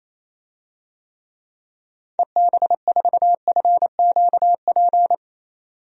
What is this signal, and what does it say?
Morse code, heard as a keyed tone, about 27 wpm, 700 Hz. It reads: EB4FQP
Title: DXCC-100 Call Signs - 6 Characters - 28wpm